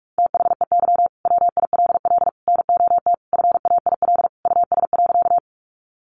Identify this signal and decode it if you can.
Morse code, heard as a keyed tone, about 44 wpm, 700 Hz. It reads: THEY WILL NOT FAIL US.
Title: Sentences from Top 1100 Words 45wpm